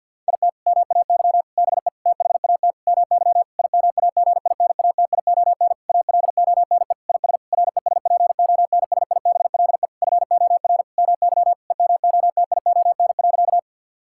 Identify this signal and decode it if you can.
Morse code, heard as a keyed tone, about 51 wpm, 700 Hz. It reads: IT MAY BE THAT MY IMAGINATION ALONE IS RESPONSIBLE FOR MY EMOTION.
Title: Sentences from Top 1600 Words 50wpm